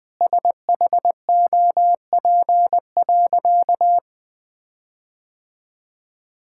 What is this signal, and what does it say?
Morse code, heard as a keyed tone, about 20 wpm, 700 Hz. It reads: SHOP.
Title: Sentences from Top 1600 Words 20wpm